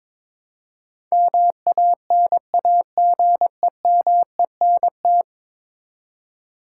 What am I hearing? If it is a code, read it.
Morse code, heard as a keyed tone, about 21 wpm, 700 Hz. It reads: MANAGEMENT
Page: morse-code-ninja-practice-22wpm